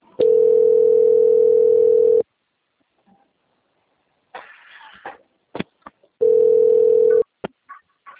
Ringback tone (North America), 440 + 480 Hz.